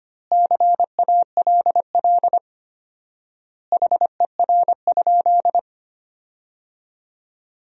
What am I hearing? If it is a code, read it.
Morse code, heard as a keyed tone, about 25 wpm, 700 Hz. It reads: CALL HER?
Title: 4x Word-Spacing - Sentences from Top 100 Words - 25wpm